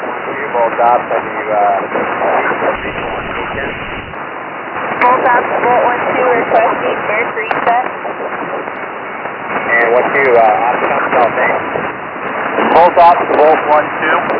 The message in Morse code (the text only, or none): none